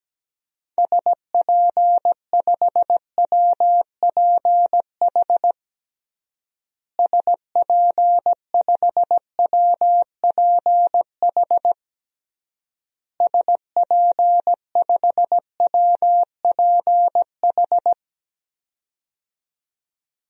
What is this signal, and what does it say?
Morse code, heard as a keyed tone, about 17 wpm, 700 Hz. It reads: SP5WPH SP5WPH SP5WPH
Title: DXCC-100 Call Signs - 6 Characters - Repeated 3x - 17wpm